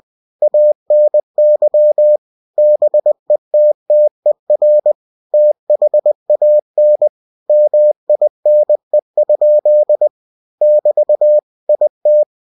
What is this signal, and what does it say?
Morse code, heard as a keyed tone, about 20 wpm, 600 Hz. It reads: ANY BETTER THAN MINE? <BT> IT
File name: cw-books / winnie-the-pooh / winnie-20wpm-page-0111